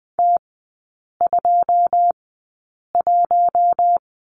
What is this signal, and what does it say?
Morse code, heard as a keyed tone, about 20 wpm, spaced as for 8 wpm, 690 Hz. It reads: T21
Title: 087 Lesson 15e - Number 5 - Three Characters 20wpm